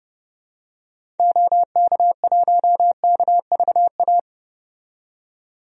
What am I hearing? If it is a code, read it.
Morse code, heard as a keyed tone, about 31 wpm, 700 Hz. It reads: OK1KVA
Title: DXCC-100 Call Signs - 6 Characters - 30wpm